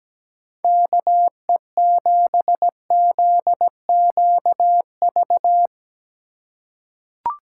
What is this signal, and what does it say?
Morse code, heard as a keyed tone, about 17 wpm, 700 Hz. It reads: KE7ZQV
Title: USA Call Signs - 6 Characters - 4x Character Spacing - 17wpm